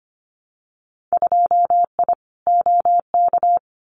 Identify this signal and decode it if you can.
Morse code, heard as a keyed tone, about 25 wpm, 700 Hz. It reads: 2I OK
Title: Winter Field Day Exchanges Encore 25wpm